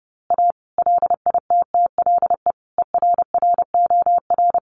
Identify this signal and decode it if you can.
Morse code, heard as a keyed tone, about 30 wpm, 700 Hz. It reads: A LITTLE ERROR
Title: A Sound Of Thunder 30wpm